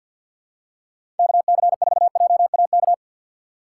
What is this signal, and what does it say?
Morse code, heard as a keyed tone, about 50 wpm, 690 Hz. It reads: KC4JAK
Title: USA Call Signs - 6 Characters - 3x Character Spacing - 50wpm